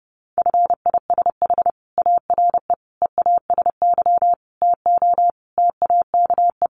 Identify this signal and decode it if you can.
Morse code, heard as a keyed tone, about 30 wpm, 700 Hz. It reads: FISH ARE EASY TO TAKE
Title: Additional Easy Phrases 30wpm